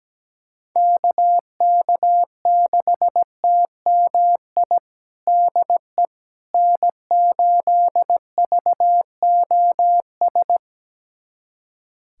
Morse code, heard as a keyed tone, about 17 wpm, 700 Hz. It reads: KK6TMI DE N8VOS
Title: USA Callsign DE Callsign 17wpm